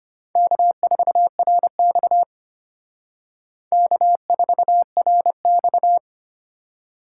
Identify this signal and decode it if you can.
Morse code, heard as a keyed tone, about 31 wpm, 700 Hz. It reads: K4RX K4RX
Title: Speed Racing - USA Call Signs 25wpm